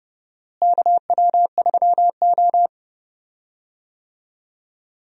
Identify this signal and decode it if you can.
Morse code, heard as a keyed tone, about 30 wpm, 700 Hz. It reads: KW3O
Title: USA Call Signs - 4 Characters - Repeated 3x - 30wpm